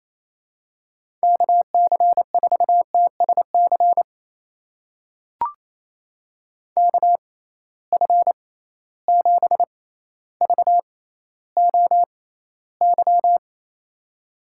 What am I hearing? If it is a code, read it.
Morse code, heard as a keyed tone, about 28 wpm, spaced as for 14 wpm, 700 Hz. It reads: KC4TSC KF7VOY
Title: USA Call Signs - 6 Characters - 6x Character Spacing - 28wpm